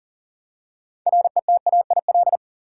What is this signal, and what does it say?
Morse code, heard as a keyed tone, about 40 wpm, 700 Hz. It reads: RETAIL
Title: Practice QSO - Exchange 3 - More Info - 40wpm